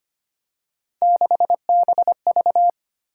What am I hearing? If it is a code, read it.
Morse code, heard as a keyed tone, about 28 wpm, 700 Hz. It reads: 6BV
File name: Speed-Racing - Three Letter-Number Combinations - 25wpm